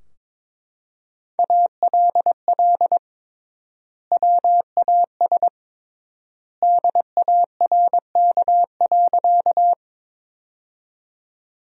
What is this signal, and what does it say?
Morse code, heard as a keyed tone, about 22 wpm, 700 Hz. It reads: ALL WAS DARK.
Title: 3x Word-Spacing - Sentences from Top 500 Words - 22wpm